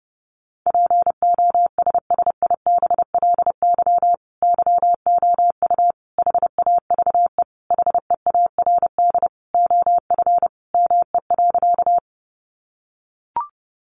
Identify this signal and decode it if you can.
Morse code, heard as a keyed tone, about 29 wpm, 700 Hz. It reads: POSSIBLY YOU HAVE HEARD OF ME.
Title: Sentences from Top 1300 Words 30wpm